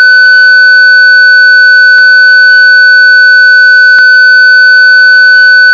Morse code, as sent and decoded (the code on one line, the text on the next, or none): none